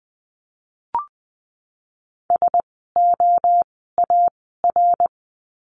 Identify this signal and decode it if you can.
Morse code, heard as a keyed tone, about 20 wpm, 700 Hz. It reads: S O A R
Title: 061 Lesson 11f - Letter h - Words 20wpm